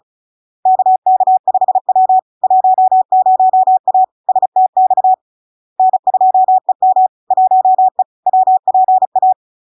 Morse code, heard as a keyed tone, about 36 wpm, 750 Hz. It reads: KK5W 10A STX N2EM 1E WPA
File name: LARK-FD-2022-Log-in-CW-35wpm-750-Hz